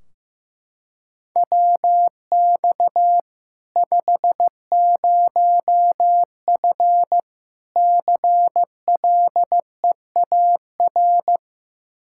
Morse code, heard as a keyed tone, about 15 wpm, 700 Hz. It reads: WX 50F CLEAR
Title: Practice QSO - Exchange 2 - Info - 15wpm